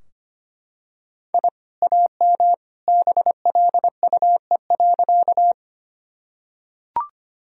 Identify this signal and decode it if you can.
Morse code, heard as a keyed tone, about 25 wpm, 700 Hz. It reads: I AM BLUE.
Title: Sentences from Top 1000 Words - Review Part 3 - 25wpm